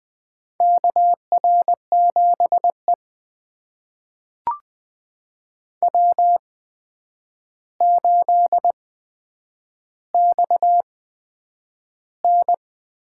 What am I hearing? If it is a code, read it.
Morse code, heard as a keyed tone, about 20 wpm, spaced as for 9 wpm, 700 Hz. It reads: KR7E W8XN
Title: USA Call Signs - 4 Characters - 8x Character Spacing - 20wpm